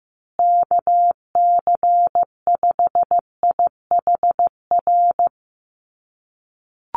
Morse code, heard as a keyed tone, about 15 wpm, 700 Hz. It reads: KC5IHR
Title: USA Call Signs Repeated 3x 15wpm